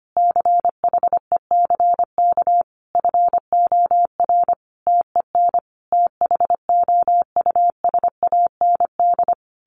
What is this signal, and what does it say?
Morse code, heard as a keyed tone, about 25 wpm, 700 Hz. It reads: CHECK FOR TEN THOUSAND
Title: A Sound Of Thunder 25wpm